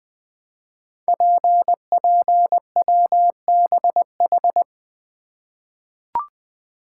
Morse code, heard as a keyed tone, about 20 wpm, 700 Hz. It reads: PPWBH
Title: Code Groups - 5 Letters - 20wpm